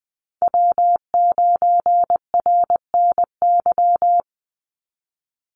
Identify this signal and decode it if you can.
Morse code, heard as a keyed tone, about 20 wpm, 700 Hz. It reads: W9RNY
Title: Call Signs - 20wpm